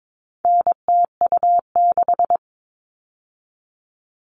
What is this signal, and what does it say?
Morse code, heard as a keyed tone, about 22 wpm, 700 Hz. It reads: NTU6
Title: All Cut Numbers - 4-Digit 22wpm